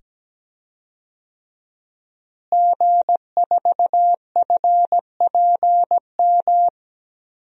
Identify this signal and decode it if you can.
Morse code, heard as a keyed tone, about 17 wpm, 700 Hz. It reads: G4FPM
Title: DXCC-100 Call Signs - 5 Characters - Repeated 2x - 17wpm